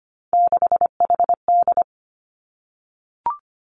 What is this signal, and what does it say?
Morse code, heard as a keyed tone, about 25 wpm, 700 Hz. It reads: 6HD
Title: Three Challenging Letter-Number Combinations 25wpm